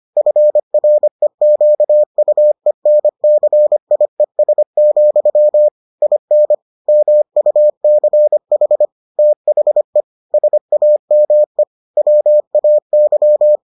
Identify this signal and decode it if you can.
Morse code, heard as a keyed tone, about 25 wpm, 600 Hz. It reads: FREQUENCIES, IN MUCH THE SAME WAY